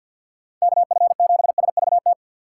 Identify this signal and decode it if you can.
Morse code, heard as a keyed tone, about 65 wpm, spaced as for 57 wpm, 700 Hz. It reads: KF7SVT
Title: Speed Racing - USA 2x3 Call Signs 50wpm